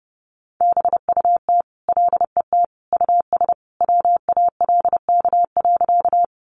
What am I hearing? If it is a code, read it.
Morse code, heard as a keyed tone, about 30 wpm, 700 Hz. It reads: BUT LET US WALK.
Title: Sentences from Top 300 Words 30wpm